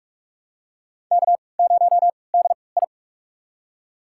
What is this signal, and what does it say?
Morse code, heard as a keyed tone, about 46 wpm, spaced as for 25 wpm, 700 Hz. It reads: K0DI
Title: USA Call Signs - 4 Characters - 3x Character Spacing - 45wpm